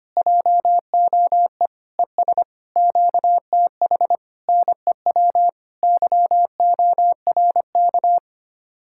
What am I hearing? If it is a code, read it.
Morse code, heard as a keyed tone, about 24 wpm, 700 Hz. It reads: JOE ES QTH NEW YORK
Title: Speed-Racing Common QSO Phrases Encore - 25wpm